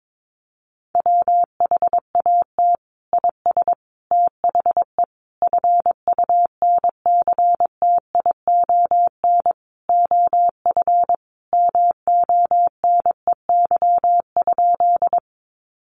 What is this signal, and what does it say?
Morse code, heard as a keyed tone, about 22 wpm, 700 Hz. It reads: WHAT IS THE FUNCTION OF MONEY?
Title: Sentences from Top 1100 Words 22wpm